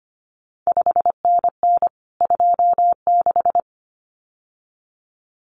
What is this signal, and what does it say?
Morse code, heard as a keyed tone, about 25 wpm, 700 Hz. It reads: 5NN 26
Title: CQ WW DX Contest Exchange Encore - 25wpm